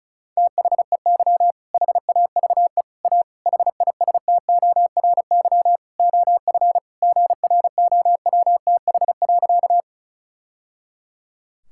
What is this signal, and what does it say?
Morse code, heard as a keyed tone, about 36 wpm, 700 Hz. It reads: THEY HAVE A HISTORY OF GROWTH.
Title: Sentences from Top 1000 Words - Review Part 5 - 35wpm